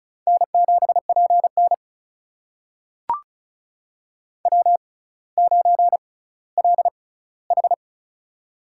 Morse code, heard as a keyed tone, about 36 wpm, spaced as for 18 wpm, 700 Hz. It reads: N7PN W9LH
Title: USA Call Signs - 4 Characters - 6x Character Spacing - 35wpm